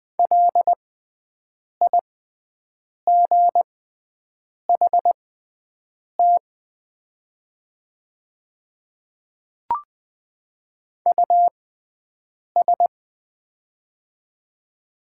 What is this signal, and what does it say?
Morse code, heard as a keyed tone, about 20 wpm, spaced as for 7 wpm, 700 Hz. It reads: LIGHT US
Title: 6x Character Spacing - Top 300 Words - 20wpm